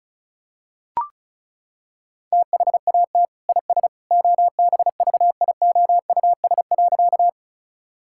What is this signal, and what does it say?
Morse code, heard as a keyed tone, about 34 wpm, 700 Hz. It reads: THAT IS OBVIOUS.
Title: Sentences from Top 1600 Words 35wpm